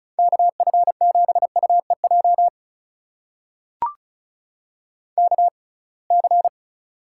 Morse code, heard as a keyed tone, about 33 wpm, spaced as for 26 wpm, 700 Hz. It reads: KF7UEJ KC
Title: USA Call Signs - 6x Character Spacing - 35wpm